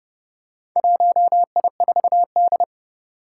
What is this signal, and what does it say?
Morse code, heard as a keyed tone, about 29 wpm, 700 Hz. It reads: 1I4D
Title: Four Challenging Letter-Number Combinations 30wpm